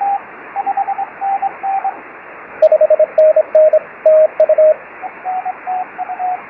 Morse code, heard as a keyed tone, about 26 wpm, 620 Hz, with radio noise. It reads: 5NN TU
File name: FO29 JG8NQJ